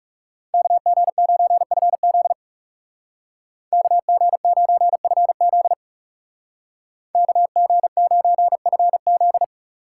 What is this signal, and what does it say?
Morse code, heard as a keyed tone, about 36 wpm, 700 Hz. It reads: KG9FZ KG9FZ KG9FZ
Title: Speed Racing - USA 2x2 Call Signs 30wpm